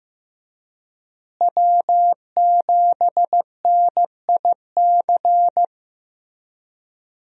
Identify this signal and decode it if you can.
Morse code, heard as a keyed tone, about 15 wpm, 700 Hz. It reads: W7NIC
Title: USA Call Signs - 5 Characters - Repeated 3x - 15wpm